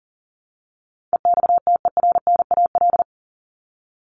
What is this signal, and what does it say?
Morse code, heard as a keyed tone, about 40 wpm, 700 Hz. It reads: EXTERNAL
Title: Top 1751-2000 Words 40wpm